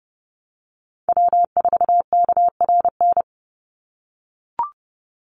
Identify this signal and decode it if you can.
Morse code, heard as a keyed tone, about 30 wpm, 700 Hz. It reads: W4KRN
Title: NA SOTA Chaser Callsigns 30wpm